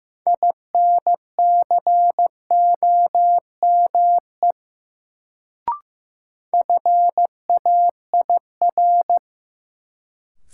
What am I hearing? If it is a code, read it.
Morse code, heard as a keyed tone, about 16 wpm, 700 Hz. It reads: INCOME FAIR
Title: Sentences from Top 1000 Words - Review Part 5 - New Word Drill - 15wpm